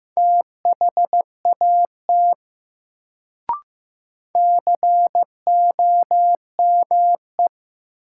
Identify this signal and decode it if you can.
Morse code, heard as a keyed tone, about 15 wpm, 700 Hz. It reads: THAT COME
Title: Sentences from Top 100 Words - New Word Drill - 15wpm